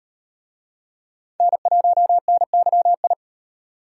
Morse code, heard as a keyed tone, about 33 wpm, 700 Hz. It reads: N1NYI